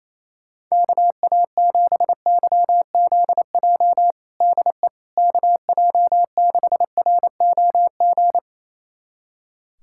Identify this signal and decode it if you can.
Morse code, heard as a keyed tone, about 28 wpm, 700 Hz. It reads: KA7YZJ DE KJ6ROG
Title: USA Callsign DE Callsign - 6 Characters - 28wpm